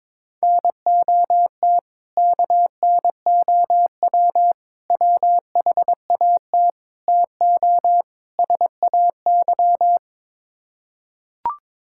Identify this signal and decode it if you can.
Morse code, heard as a keyed tone, about 22 wpm, 700 Hz. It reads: NOT KNOW WHAT TO SAY
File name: practice / 22wpm / Build A Phrase - 5-Word Phrases - 22wpm